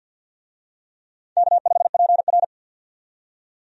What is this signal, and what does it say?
Morse code, heard as a keyed tone, about 53 wpm, 700 Hz. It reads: K5PR
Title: USA Call Signs - 4 Characters - 50wpm